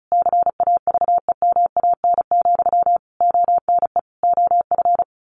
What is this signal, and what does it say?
Morse code, heard as a keyed tone, about 34 wpm, 700 Hz. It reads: CAVEMAN, ONE OF
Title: A Sound Of Thunder 35wpm